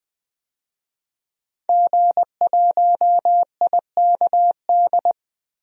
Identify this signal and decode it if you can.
Morse code, heard as a keyed tone, about 20 wpm, 700 Hz. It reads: G1IKD
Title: DXCC-100 Call Signs - 5 Characters - 20wpm